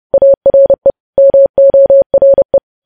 Morse code, heard as a keyed tone, about 30 wpm, 550 Hz. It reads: ARE MORE
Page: news headlines in morse code at 30 wpm